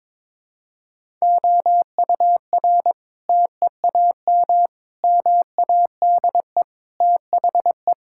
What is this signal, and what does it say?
Morse code, heard as a keyed tone, about 22 wpm, 700 Hz. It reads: OUR TEAM MADE THE
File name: Sentences with 7-Word Phrases - 22wpm